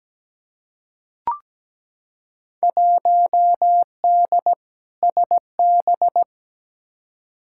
Morse code, heard as a keyed tone, about 17 wpm, 700 Hz. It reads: 1D SB
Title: Field Day Exchanges - 17wpm